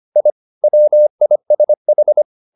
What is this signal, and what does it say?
Morse code, heard as a keyed tone, about 25 wpm, 600 Hz. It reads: I WISH
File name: the old man and the sea part 1 25wpm